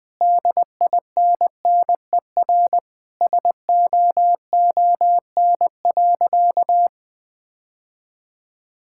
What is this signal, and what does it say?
Morse code, heard as a keyed tone, about 21 wpm, 700 Hz. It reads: DINNER SOON.
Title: Sentences with 3-Word Phrases - Spoken Beginning - 20wpm